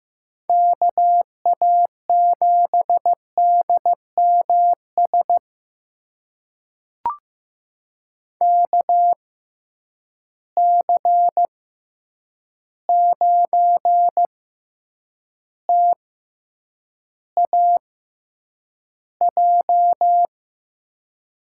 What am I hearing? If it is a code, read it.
Morse code, heard as a keyed tone, about 15 wpm, spaced as for 8 wpm, 700 Hz. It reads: KA7DMS KC9TAJ
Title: USA Call Signs - 6 Characters - 6x Character Spacing - 15wpm